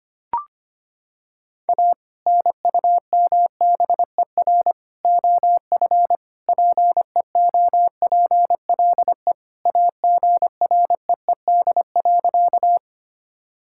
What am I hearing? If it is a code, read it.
Morse code, heard as a keyed tone, about 25 wpm, 700 Hz. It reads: A NUMBER OF PEOPLE AGREED.
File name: practice / 25wpm / Sentences with 3-Word Phrases - 25wpm